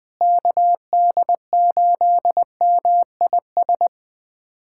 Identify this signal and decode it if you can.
Morse code, heard as a keyed tone, about 21 wpm, 700 Hz. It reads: KD8MIS
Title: USA Callsign DE Callsign 20wpm